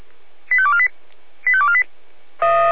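Five-tone selective call (ZVEI), heard as quick runs of five tones, tones of about 70 ms. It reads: 85248 85248
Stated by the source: ZVEI five-tone selective call received on an SDR.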